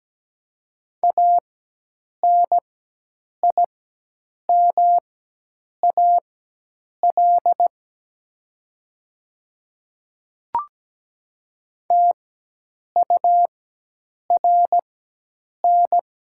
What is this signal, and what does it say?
Morse code, heard as a keyed tone, about 17 wpm, spaced as for 8 wpm, 700 Hz. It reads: ANIMAL TURN
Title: 4x Character Spacing - Top 500 Words - 17wpm